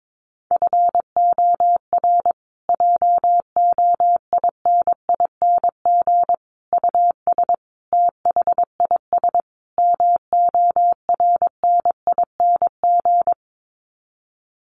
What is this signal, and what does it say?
Morse code, heard as a keyed tone, about 22 wpm, 700 Hz. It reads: FOR JOINING US THIS MORNING
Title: Sets of 5 Words Encore 22wpm